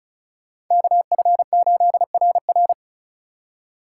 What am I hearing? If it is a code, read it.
Morse code, heard as a keyed tone, about 46 wpm, 700 Hz. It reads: KF8RR